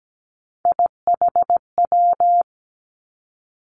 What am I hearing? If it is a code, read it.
Morse code, heard as a keyed tone, about 17 wpm, 700 Hz. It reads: IHW
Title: Three Challenging Letter-Number Combinations 17wpm